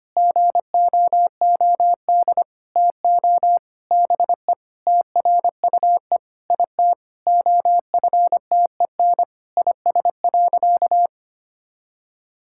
Morse code, heard as a keyed tone, about 25 wpm, 700 Hz. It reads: GOOD TO BE TRUE IT OFTEN IS.
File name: practice / 25wpm / Sentences with 9-Word Phrases - 25wpm